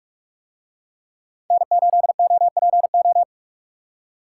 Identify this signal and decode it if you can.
Morse code, heard as a keyed tone, about 48 wpm, 700 Hz. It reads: N8OPO